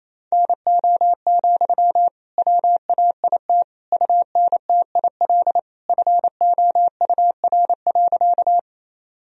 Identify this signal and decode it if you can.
Morse code, heard as a keyed tone, about 28 wpm, 700 Hz. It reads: NO, WAIT UNTIL FOUR.
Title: Sentences from Top 1000 Words - Review Part 1 - 28wpm